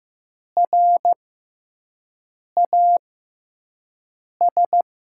Morse code, heard as a keyed tone, about 15 wpm, spaced as for 5 wpm, 700 Hz. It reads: RAS
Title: USA Call Signs - 6x Character Spacing - 15wpm